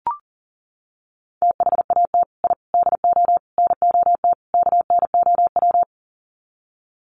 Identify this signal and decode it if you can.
Morse code, heard as a keyed tone, about 40 wpm, 700 Hz. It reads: THAT I DO NOT KNOW
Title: Sets of 5 Words Encore2 40wpm